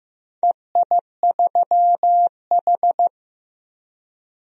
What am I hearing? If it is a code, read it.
Morse code, heard as a keyed tone, about 15 wpm, 700 Hz. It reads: EI3H
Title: DXCC-100 Call Signs - 4 Characters - 15wpm